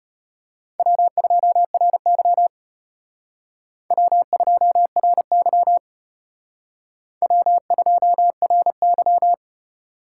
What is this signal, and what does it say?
Morse code, heard as a keyed tone, about 32 wpm, 700 Hz. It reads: W2RY W2RY W2RY